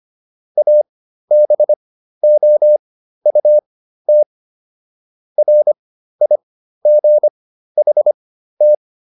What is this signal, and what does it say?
Morse code, heard as a keyed tone, about 25 wpm, spaced as for 13 wpm, 600 Hz. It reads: ABOUT RIGHT